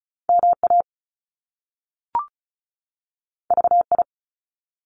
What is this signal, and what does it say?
Morse code, heard as a keyed tone, about 34 wpm, 700 Hz. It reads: M A V I
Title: Top Two Letter Combinations in Words 35wpm